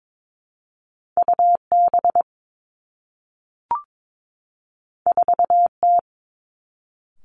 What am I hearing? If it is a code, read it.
Morse code, heard as a keyed tone, about 22 wpm, 700 Hz. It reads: U B 4 T
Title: Two Challenging Letter-Number Combinations 22wpm